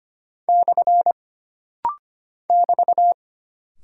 Morse code, heard as a keyed tone, about 25 wpm, 700 Hz. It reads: / <BT>
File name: Single Punctuation-Prosign - Rapid-Fire - 25wpm